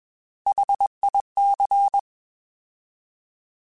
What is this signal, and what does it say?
Morse code, heard as a keyed tone, about 21 wpm, 800 Hz. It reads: HIC
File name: Alphabetical / Alphabetical mp3s / Hs @20